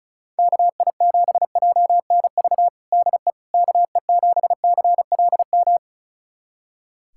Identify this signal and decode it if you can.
Morse code, heard as a keyed tone, about 34 wpm, 700 Hz. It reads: KI7JNV DE KE7CLM
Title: USA Callsign DE Callsign - 6 Characters - 35wpm